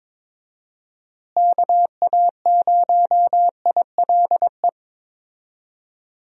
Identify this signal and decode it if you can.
Morse code, heard as a keyed tone, about 22 wpm, 700 Hz. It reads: KA0ILE
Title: USA Call Signs - 6 Characters - Repeated 2x - 22wpm